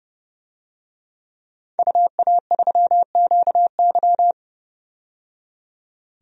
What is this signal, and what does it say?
Morse code, heard as a keyed tone, about 32 wpm, 700 Hz. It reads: UA3QY
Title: DXCC-100 Call Signs - 5 Characters - 30wpm